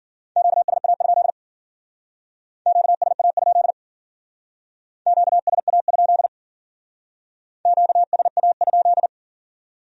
Morse code, heard as a keyed tone, about 52 wpm, 700 Hz. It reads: QSA? QSA? QSA? QSA?